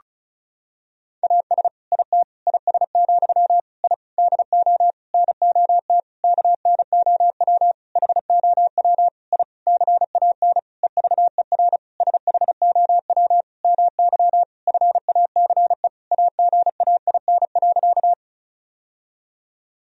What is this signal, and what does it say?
Morse code, heard as a keyed tone, about 36 wpm, 700 Hz. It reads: AS IT IS, I DO NOT KNOW HOW I CAN EVER SHOW MY FACE AGAIN.
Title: Sentences from Top 1000 Words - Review Part 2 - 35wpm